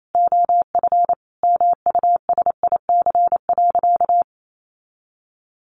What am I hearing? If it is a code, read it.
Morse code, heard as a keyed tone, about 28 wpm, 700 Hz. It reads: OF MUSIC.
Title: Sentences from Top 1200 Words 28wpm